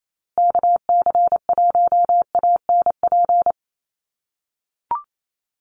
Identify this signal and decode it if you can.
Morse code, heard as a keyed tone, about 28 wpm, 700 Hz. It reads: KC1ANP
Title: USA Call Signs Repeated 2x 28wpm